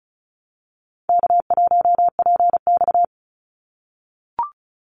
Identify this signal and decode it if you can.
Morse code, heard as a keyed tone, about 36 wpm, 700 Hz. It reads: K1PX
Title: Call Signs - 35wpm